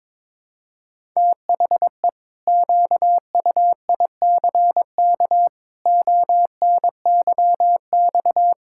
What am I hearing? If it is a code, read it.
Morse code, heard as a keyed tone, about 22 wpm, 700 Hz. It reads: THE QUICK ONYX
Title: Pangram Challenge v2 - 22wpm